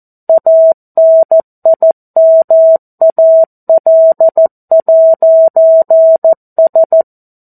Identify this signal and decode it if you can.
Morse code, heard as a keyed tone, about 14 wpm, 650 Hz. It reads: ANIMAL'S
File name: 02-Chapter-02 14 WPM